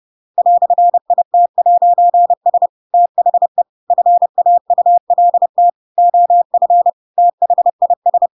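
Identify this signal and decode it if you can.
Morse code, heard as a keyed tone, about 30 wpm, 700 Hz. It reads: "IT'S THE FAULT OF THIS
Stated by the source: Chapter 2 of The Adventures of Pinocchio read as a book in Morse code (CW) at 30 wpm, by ik1hge.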